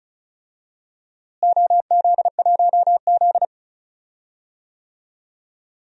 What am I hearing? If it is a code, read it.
Morse code, heard as a keyed tone, about 36 wpm, 700 Hz. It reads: OZ1Z
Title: DXCC-100 Call Signs - 4 Characters - Repeated 2x - 35wpm